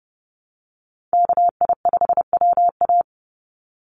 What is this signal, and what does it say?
Morse code, heard as a keyed tone, about 30 wpm, 700 Hz. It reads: KI5WA
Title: NA SOTA Chaser Callsigns 30wpm